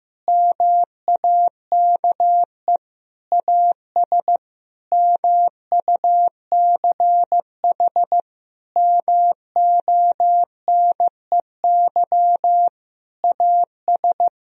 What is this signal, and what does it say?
Morse code, heard as a keyed tone, about 16 wpm, 700 Hz. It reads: MAKE AS MUCH MONEY AS
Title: Sentences from Top 1000 Words - Review Part 3 - 15wpm